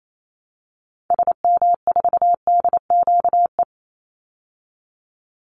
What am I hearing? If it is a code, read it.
Morse code, heard as a keyed tone, about 28 wpm, 690 Hz. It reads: SM4DQE
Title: Call Signs Repeated 3x - 28wpm